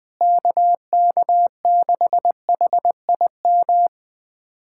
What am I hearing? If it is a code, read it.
Morse code, heard as a keyed tone, about 21 wpm, 700 Hz. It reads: KK6HIM
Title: USA Callsign DE Callsign - 6 Characters - 20wpm